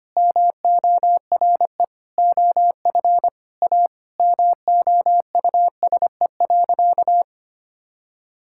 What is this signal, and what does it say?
Morse code, heard as a keyed tone, about 25 wpm, 700 Hz. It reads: MORE OF A MOUSE.
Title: Sentences from Top 1000 Words - Review Part 5 - 25wpm